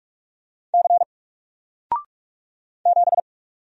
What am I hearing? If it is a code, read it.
Morse code, heard as a keyed tone, about 44 wpm, 700 Hz. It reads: C 7
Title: Single Letter-Number Alt 45wpm